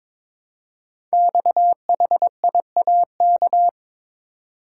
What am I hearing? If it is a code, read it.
Morse code, heard as a keyed tone, about 22 wpm, 700 Hz. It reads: XHIAK
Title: Code Groups - 5 Letter:Numbers - 22wpm